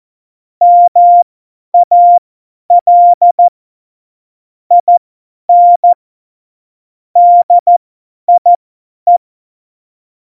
Morse code, heard as a keyed tone, about 14 wpm, spaced as for 10 wpm, 700 Hz. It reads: MAL IN DIE
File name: Morsekurs / Witze / Witz 11. September 2018, 50 F70